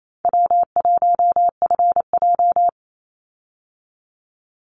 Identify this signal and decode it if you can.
Morse code, heard as a keyed tone, about 28 wpm, 700 Hz. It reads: W1FJ
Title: Easy Call Signs Repeated 3x - 28wpm